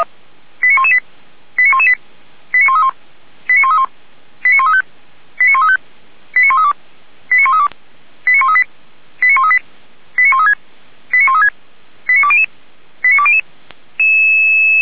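Five-tone selective call (ZVEI), heard as quick runs of five tones, tones of about 70 ms. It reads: 89118 89118 89131 89131 89135 89135 89132 89132 89138 89138 89136 89136 89200 89200
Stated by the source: Five-tone ZVEI selective call received by SDR.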